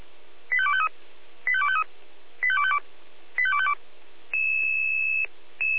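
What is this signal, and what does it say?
Five-tone selective call (ZVEI), heard as quick runs of five tones, tones of about 70 ms. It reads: 85253 85253 85252 85252
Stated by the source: ZVEI five-tone selective call 85230, received on SDR.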